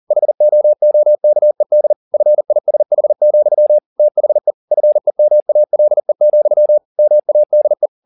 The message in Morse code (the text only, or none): HOOKED FISH, THE FEMALE, MADE